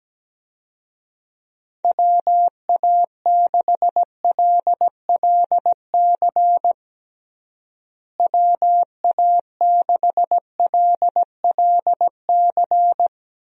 Morse code, heard as a keyed tone, about 17 wpm, 700 Hz. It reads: WA6LLC WA6LLC
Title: USA Call Signs - 6 Characters - Repeated 3x - 17wpm